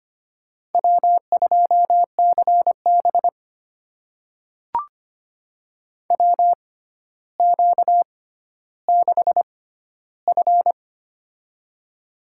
Morse code, heard as a keyed tone, about 25 wpm, spaced as for 13 wpm, 700 Hz. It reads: W2CB WQ6F
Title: USA Call Signs - 4 Characters - 6x Character Spacing - 25wpm